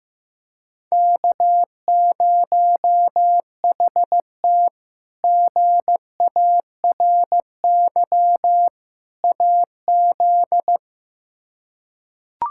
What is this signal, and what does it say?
Morse code, heard as a keyed tone, about 16 wpm, 700 Hz. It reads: K0HT GARY AZ
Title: K1USN SST QSO Practice - 15wpm